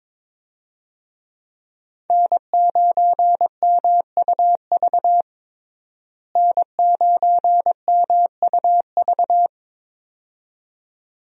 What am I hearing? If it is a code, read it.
Morse code, heard as a keyed tone, about 21 wpm, 700 Hz. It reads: N9MUV N9MUV
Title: USA Call Signs - 5 Characters - Repeated 2x - 22wpm